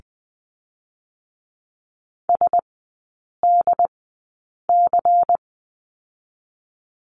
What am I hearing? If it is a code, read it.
Morse code, heard as a keyed tone, about 20 wpm, spaced as for 8 wpm, 700 Hz. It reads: SDC
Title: 127 Lesson 21e - Number 6 - Three Characters 20wpm